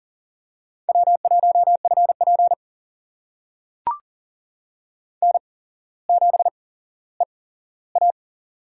Morse code, heard as a keyed tone, about 40 wpm, spaced as for 17 wpm, 700 Hz. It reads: W1FP N7EA
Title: USA Call Signs - 4 Characters - 8x Character Spacing - 40wpm